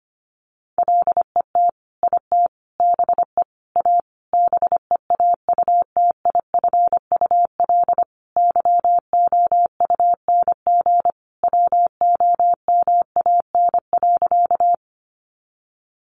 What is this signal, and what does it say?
Morse code, heard as a keyed tone, about 25 wpm, 700 Hz. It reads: LET IT BE A BEAUTIFUL YOUNG WOMAN.
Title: Sentences from Top 1100 Words 25wpm